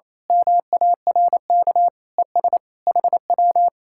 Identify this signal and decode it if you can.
Morse code, heard as a keyed tone, about 26 wpm, 700 Hz. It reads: MARK ES HW
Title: Speed-Racing Common QSO Phrases - 28wpm